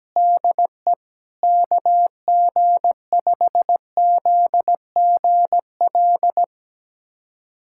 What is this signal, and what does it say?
Morse code, heard as a keyed tone, about 17 wpm, 700 Hz. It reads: DE KG5ZGL
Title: USA Callsign DE Callsign - 6 Characters - 17wpm